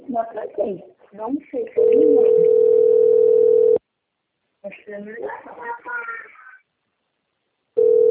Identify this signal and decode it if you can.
Ringback tone (North America), 440 + 480 Hz, with speech.
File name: out- - - - - .7205